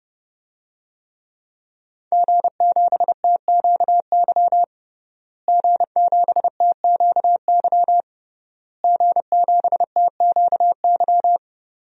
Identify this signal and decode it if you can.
Morse code, heard as a keyed tone, about 32 wpm, 700 Hz. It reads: G7TQY G7TQY G7TQY
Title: DXCC-100 Call Signs - 5 Characters - Repeated 3x - 30wpm